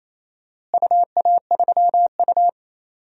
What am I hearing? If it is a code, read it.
Morse code, heard as a keyed tone, about 28 wpm, 700 Hz. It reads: UA3U
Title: DXCC-100 Call Signs - 4 Characters - Repeated 2x - 28wpm